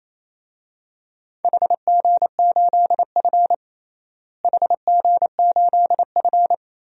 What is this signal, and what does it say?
Morse code, heard as a keyed tone, about 28 wpm, 700 Hz. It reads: HG8F HG8F
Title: DXCC-100 Call Signs - 4 Characters - Repeated 2x - 28wpm